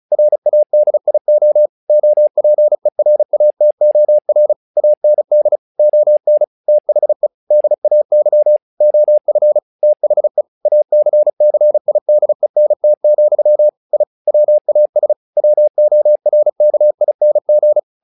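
Morse code, heard as a keyed tone, about 36 wpm, 600 Hz. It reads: RADIO OPERATOR AND ON THE DAY OF THE ACCIDENT, I WAS WORKING
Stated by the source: Morse practice text "Tower Trouble" sent at 35 WPM, from the SKCC - Straight Key Century Club learning center audio files.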